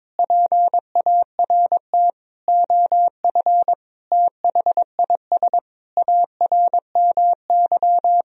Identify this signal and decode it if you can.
Morse code, heard as a keyed tone, about 22 wpm, 700 Hz. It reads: PART OF THIS ARMY
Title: Sentences from Top 1000 Words - Review Part 2 - 22wpm